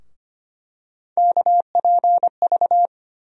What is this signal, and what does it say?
Morse code, heard as a keyed tone, about 25 wpm, 700 Hz. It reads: KPV
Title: USA Call Sign Postfixes - 3 Letters 25wpm